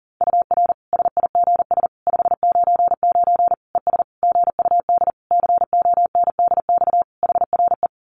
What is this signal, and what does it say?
Morse code, heard as a keyed tone, about 40 wpm, 700 Hz. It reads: UR SIGS 599 ES GUD CONDX HRE
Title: Common QSO Phrases 40wpm